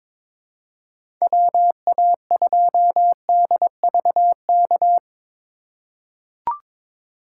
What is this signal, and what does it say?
Morse code, heard as a keyed tone, about 22 wpm, 700 Hz. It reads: WA2DVK
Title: USA Call Signs - 6 Characters - 4x Character Spacing - 22wpm